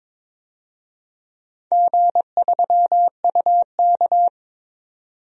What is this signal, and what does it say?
Morse code, heard as a keyed tone, about 22 wpm, 700 Hz. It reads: G3UK
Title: DXCC-100 Call Signs - 4 Characters - 22wpm